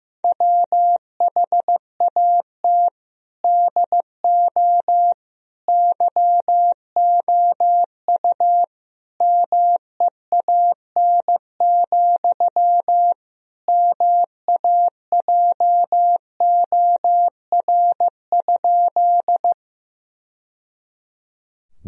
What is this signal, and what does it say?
Morse code, heard as a keyed tone, about 16 wpm, 700 Hz. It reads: WHAT DO YOU MEAN, MAJOR?
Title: Sentences from Top 1000 Words - Review Part 4 - 15wpm